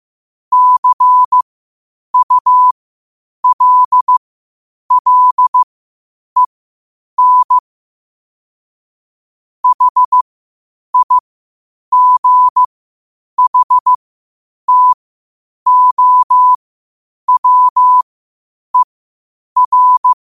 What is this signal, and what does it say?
Morse code, heard as a keyed tone, about 15 wpm, spaced as for 8 wpm, 1000 Hz. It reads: CULLEN HIGHTOWER
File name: qotdfiles / 8wpm